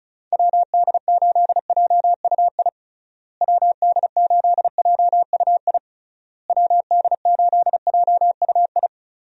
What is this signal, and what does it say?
Morse code, heard as a keyed tone, about 36 wpm, 700 Hz. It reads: WD8JUI WD8JUI WD8JUI
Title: USA Call Signs - 6 Characters - Repeated 3x - 35wpm